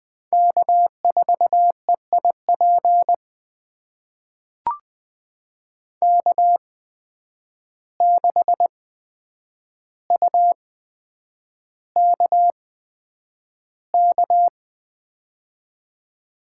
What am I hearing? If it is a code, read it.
Morse code, heard as a keyed tone, about 20 wpm, spaced as for 9 wpm, 700 Hz. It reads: K4EIP K6UKK
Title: USA Call Signs - 5 Characters - 8x Character Spacing - 20wpm